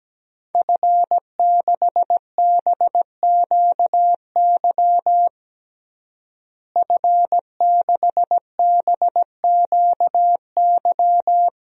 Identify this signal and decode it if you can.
Morse code, heard as a keyed tone, about 17 wpm, 700 Hz. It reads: F6BQY F6BQY
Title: DXCC-100 Call Signs - 5 Characters - Repeated 3x - 17wpm